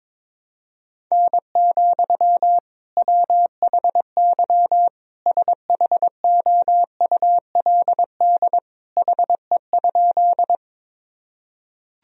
Morse code, heard as a keyed tone, about 22 wpm, 700 Hz. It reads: N, WHY SHOULD HE?
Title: Sentences from Top 300 Words and Abbreviations 22wpm